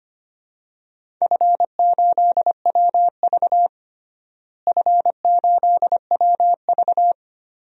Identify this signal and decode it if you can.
Morse code, heard as a keyed tone, about 25 wpm, 700 Hz. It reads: F8WV F8WV
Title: DXCC-100 Call Signs - 4 Characters - Repeated 3x - 25wpm